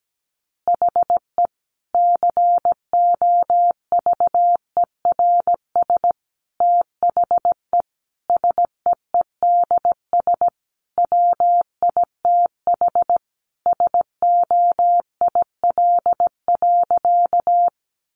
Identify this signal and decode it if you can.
Morse code, heard as a keyed tone, about 17 wpm, 700 Hz. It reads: HE COVERS THE SEEDS WITH SOIL.
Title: Easy Phrases 17wpm